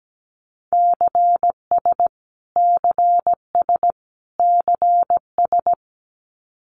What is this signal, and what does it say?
Morse code, heard as a keyed tone, about 17 wpm, 700 Hz. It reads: CS CS CS
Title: Abbreviations Repeated 3x - 17wpm